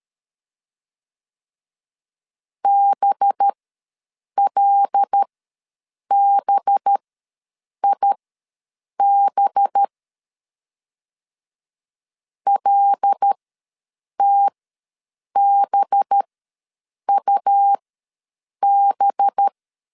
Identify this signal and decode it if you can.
Morse code, heard as a keyed tone, about 13 wpm, spaced as for 7 wpm, 800 Hz. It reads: BLBIB LTBUB
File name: 5 LB ws 12wpm